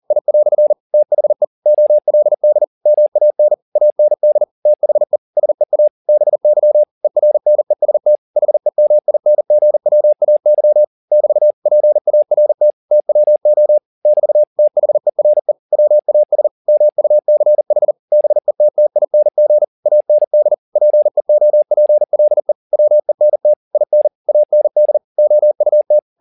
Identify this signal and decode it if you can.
Morse code, heard as a keyed tone, about 40 wpm, 600 Hz. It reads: I" THE OLD MAN AND THE SEA BY ERNEST HEMINGWAY <BT> PART TWO <BT> THERE WAS MUCH BETTING AND PEOPLE WENT IN AND OUT
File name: the old man and the sea 2 40wpm